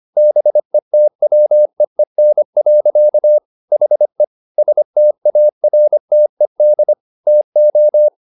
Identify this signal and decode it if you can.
Morse code, heard as a keyed tone, about 25 wpm, 600 Hz. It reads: BETWEEN. HE STARTED TO